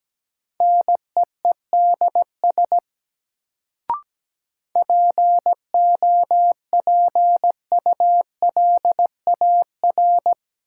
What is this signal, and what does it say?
Morse code, heard as a keyed tone, about 17 wpm, 700 Hz. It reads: NEEDS POPULAR
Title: Sentences from Top 1000 Words - Review Part 3 - New Word Drill - 17wpm